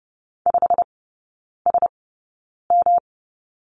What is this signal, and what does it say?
Morse code, heard as a keyed tone, about 30 wpm, spaced as for 9 wpm, 700 Hz. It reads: 5SM
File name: Lesson 18e - Letter m - Three Characters 30wpm